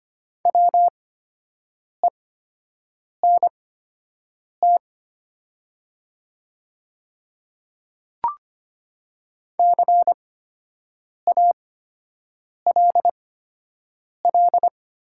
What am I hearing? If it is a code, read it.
Morse code, heard as a keyed tone, about 25 wpm, spaced as for 7 wpm, 700 Hz. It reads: WENT CALL
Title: 8x Character Spacing - Top 200 Words - 25wpm